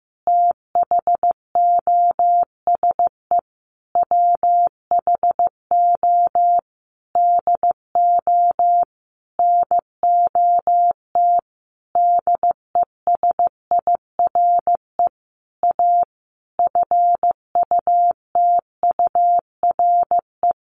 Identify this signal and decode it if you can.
Morse code, heard as a keyed tone, about 15 wpm, 700 Hz. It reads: THOSE WHO DO NOT DESIRE A FUTURE
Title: Sentences from Top 1500 Words 15wpm